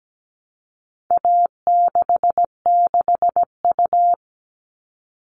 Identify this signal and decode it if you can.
Morse code, heard as a keyed tone, about 17 wpm, 700 Hz. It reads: A66U
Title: All Cut Numbers - 4-Digit 17wpm